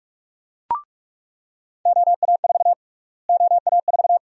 Morse code, heard as a keyed tone, about 41 wpm, 700 Hz. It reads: OA4 OA4
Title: USA Call Sign Prefixes - Repeated 2x - 45wpm